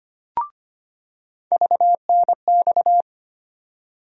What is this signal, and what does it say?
Morse code, heard as a keyed tone, about 25 wpm, 700 Hz. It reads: VNX
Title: USA Call Sign Postfixes - 3 Letters 25wpm